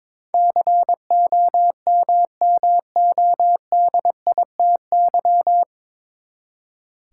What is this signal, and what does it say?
Morse code, heard as a keyed tone, about 22 wpm, 700 Hz. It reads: COMMODITY
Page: Morse Code Ninja practice 22wpm